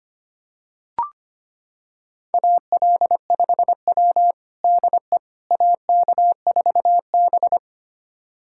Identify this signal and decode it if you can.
Morse code, heard as a keyed tone, about 25 wpm, 700 Hz. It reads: AL5W DE AK4B
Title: USA Callsign DE Callsign - 4 Characters - 25wpm